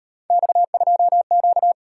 Morse code, heard as a keyed tone, about 37 wpm, 700 Hz. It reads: X2Q